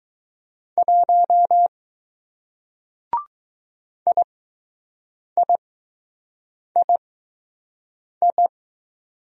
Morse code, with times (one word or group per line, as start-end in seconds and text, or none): 0.8-1.7 1
4.1-8.5 IIII